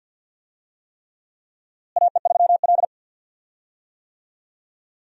Morse code, heard as a keyed tone, about 51 wpm, 700 Hz. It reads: AE3L
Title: USA Call Signs - 4 Characters - Repeated 3x - 50wpm